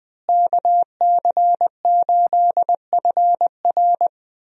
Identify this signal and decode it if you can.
Morse code, heard as a keyed tone, about 21 wpm, 700 Hz. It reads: KC8FR
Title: USA Call Signs - 5 Characters - Repeated 2x - 20wpm